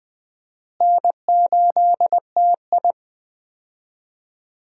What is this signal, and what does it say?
Morse code, heard as a keyed tone, about 20 wpm, 700 Hz. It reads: N8TI
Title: USA Call Signs - 4 Characters - Repeated 2x - 20wpm